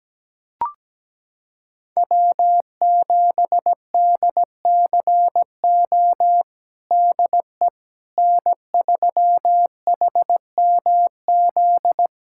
Morse code, heard as a keyed tone, about 17 wpm, 700 Hz. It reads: W7DCO DE N3HMZ
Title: USA Callsign DE Callsign - 5 Characters - 17wpm